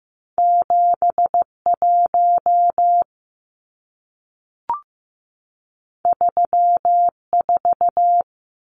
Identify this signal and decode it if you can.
Morse code, heard as a keyed tone, about 15 wpm, 700 Hz. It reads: 71 34
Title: Numbers - 2-Digit 15wpm